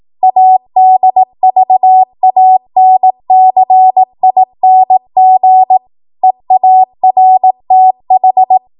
Morse code, heard as a keyed tone, about 18 wpm, 750 Hz. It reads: ADVANCING EARTH